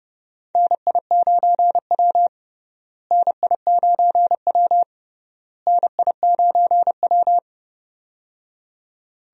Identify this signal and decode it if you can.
Morse code, heard as a keyed tone, about 30 wpm, 700 Hz. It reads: NI9W NI9W NI9W
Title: USA Call Signs - 4 Characters - Repeated 3x - 30wpm